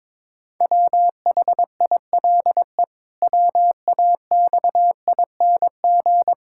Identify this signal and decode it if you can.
Morse code, heard as a keyed tone, about 22 wpm, 700 Hz. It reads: WHILE WAXING
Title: Pangram Challenge v2 - 22wpm